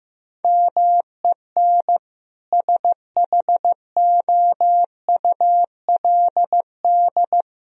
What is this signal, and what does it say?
Morse code, heard as a keyed tone, about 15 wpm, 700 Hz. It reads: MEN SHOULD
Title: Sentences from Top 1000 Words - Review Part 3 - 15wpm